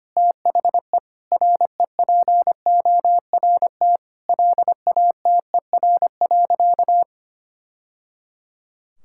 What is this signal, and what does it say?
Morse code, heard as a keyed tone, about 25 wpm, 700 Hz. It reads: THE REPORT LATER.
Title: Sentences from Top 1000 Words - Review Part 6 - 25wpm